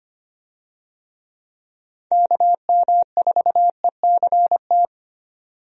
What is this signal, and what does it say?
Morse code, heard as a keyed tone, about 25 wpm, 700 Hz. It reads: KM4ECT
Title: USA Call Signs - 6 Characters - 25wpm